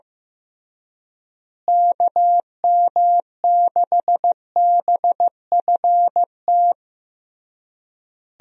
Morse code, heard as a keyed tone, about 16 wpm, 700 Hz. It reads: KM6BFT
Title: USA Call Signs - 6 Characters - Repeated 3x - 15wpm